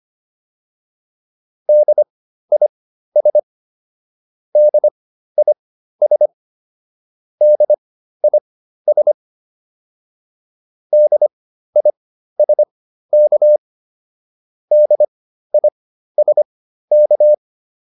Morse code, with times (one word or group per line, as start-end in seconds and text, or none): 1.7-3.4 DIS
4.6-6.3 DIS
7.4-9.1 DIS
10.9-13.6 DISK
14.7-17.3 DISK